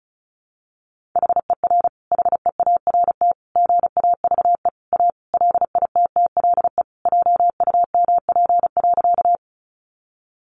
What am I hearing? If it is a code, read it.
Morse code, heard as a keyed tone, about 34 wpm, 700 Hz. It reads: HER HEART GAVE A LITTLE JUMP.
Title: Sentences from Top 1300 Words 35wpm